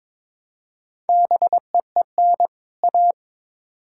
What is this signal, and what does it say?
Morse code, heard as a keyed tone, about 21 wpm, 700 Hz. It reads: BEEN A
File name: Build A Phrase - 2-Word Phrases - 22wpm